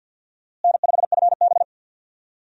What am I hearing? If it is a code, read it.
Morse code, heard as a keyed tone, about 46 wpm, 700 Hz. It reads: N5FB
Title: USA Call Signs - 4 Characters - Repeated 2x - 50wpm